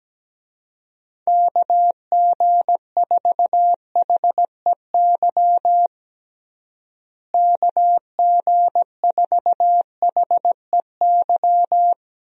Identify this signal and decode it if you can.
Morse code, heard as a keyed tone, about 17 wpm, 700 Hz. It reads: KG4HEY KG4HEY
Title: USA Call Signs - 6 Characters - Repeated 3x - 17wpm